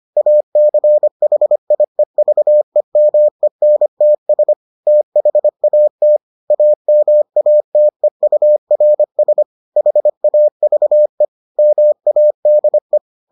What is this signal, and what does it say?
Morse code, heard as a keyed tone, about 25 wpm, 600 Hz. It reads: ACHIEVEMENTS THAT AMATEURS HAVE MADE